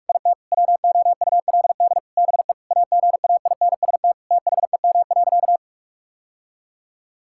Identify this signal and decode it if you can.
Morse code, heard as a keyed tone, about 46 wpm, 700 Hz. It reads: IT WOULD BE AGAINST THEM.
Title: Sentences from Top 1000 Words - Review Part 6 - 45wpm